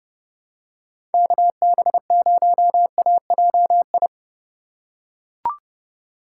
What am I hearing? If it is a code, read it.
Morse code, heard as a keyed tone, about 30 wpm, 700 Hz. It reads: KB0AJI
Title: USA Call Signs - 6 Characters - 4x Character Spacing - 30wpm